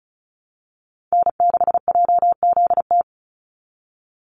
Morse code, heard as a keyed tone, about 34 wpm, 700 Hz. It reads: N6JZT
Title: NA SOTA Chaser Callsigns 35wpm